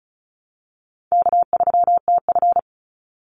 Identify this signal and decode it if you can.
Morse code, heard as a keyed tone, about 34 wpm, 700 Hz. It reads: K3TF
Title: Easy Call Signs Repeated 2x - 35wpm